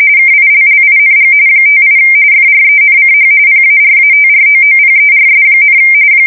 RTTY, about 45.45 baud. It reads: KILLING 58 FRENCH PARATROOPERS,
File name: fsk170 / USS-New-Jersey